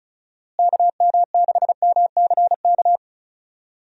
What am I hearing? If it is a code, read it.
Morse code, heard as a keyed tone, about 36 wpm, 700 Hz. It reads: KM6MCK
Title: USA Call Signs - 6 Characters - Repeated 2x - 35wpm